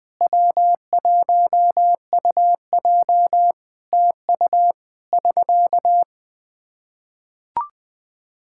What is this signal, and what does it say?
Morse code, heard as a keyed tone, about 20 wpm, 700 Hz. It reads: W1UJ TU <SK>
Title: Practice QSO - Exchange 4 - Ending - 20wpm